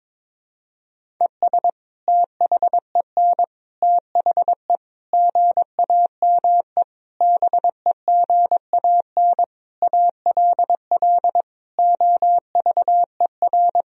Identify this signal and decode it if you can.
Morse code, heard as a keyed tone, about 22 wpm, 700 Hz. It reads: ES THEN THE GAME BEGAN ALL OVER
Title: Sentences from Top 500 Words and Abbreviations 22wpm